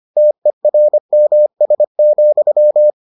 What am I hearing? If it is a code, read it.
Morse code, heard as a keyed tone, about 25 wpm, 600 Hz. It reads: TERMS,